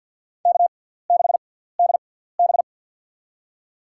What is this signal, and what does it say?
Morse code, heard as a keyed tone, about 50 wpm, spaced as for 17 wpm, 700 Hz. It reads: K6DB
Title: USA Call Signs - 4 Characters - 6x Character Spacing - 50wpm